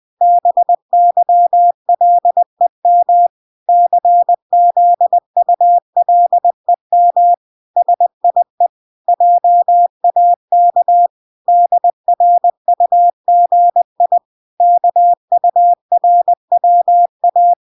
Morse code, heard as a keyed tone, about 20 wpm, 700 Hz. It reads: BYLEM CZULEM SIE JAK DRUGI KURWA